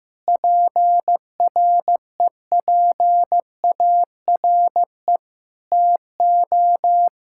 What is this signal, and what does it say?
Morse code, heard as a keyed tone, about 15 wpm, 700 Hz. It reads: PREPARE TO
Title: Sentences from Top 1000 Words - Review Part 6 - 15wpm